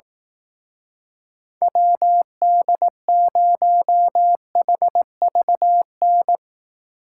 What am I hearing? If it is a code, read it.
Morse code, heard as a keyed tone, about 18 wpm, 700 Hz. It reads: WD0HVN
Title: Speed Racing - USA Call Signs 15wpm